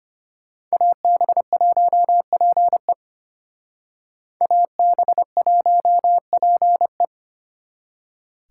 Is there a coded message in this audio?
Yes: Morse code from 0.7 to 7.0 s, heard as a keyed tone, about 32 wpm, 700 Hz. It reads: AB1PE AB1PE